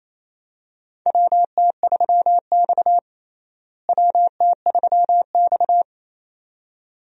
Morse code, heard as a keyed tone, about 28 wpm, 700 Hz. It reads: WT3X WT3X
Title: USA Call Signs - 4 Characters - Repeated 2x - 28wpm